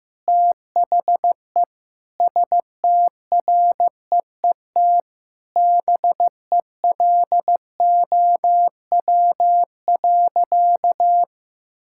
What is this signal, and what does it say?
Morse code, heard as a keyed tone, about 16 wpm, 700 Hz. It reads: THE STREET BELOW.
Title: Sentences from Top 1000 Words - Review Part 5 - 15wpm